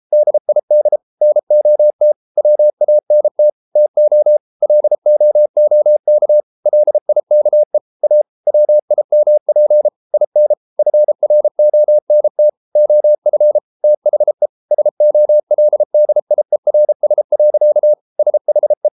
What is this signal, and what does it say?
Morse code, heard as a keyed tone, about 33 wpm, 600 Hz. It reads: DID NOT WANT TO LOOK LIKE A WIMP IN FRONT OF THE SOLDIERS. SHE